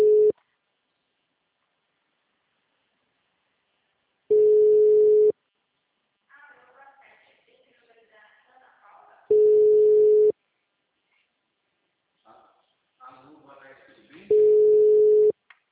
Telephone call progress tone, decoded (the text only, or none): ringback tone